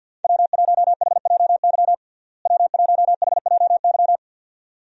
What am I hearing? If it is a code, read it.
Morse code, heard as a keyed tone, about 50 wpm, 700 Hz. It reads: W1HJY W1HJY
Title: USA Call Signs - 5 Characters - Repeated 2x - 50wpm